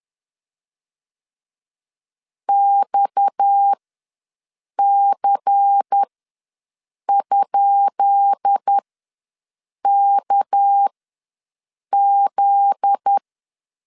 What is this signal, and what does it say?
Morse code, heard as a keyed tone, about 10 wpm, spaced as for 6 wpm, 800 Hz. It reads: XC?KZ
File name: LEZIONI / 10 SIGNS / 10B INT ws 10wpm 4134